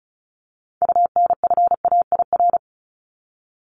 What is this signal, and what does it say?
Morse code, heard as a keyed tone, about 34 wpm, 700 Hz. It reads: UNFAIR
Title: Top 1751-2000 Words 35wpm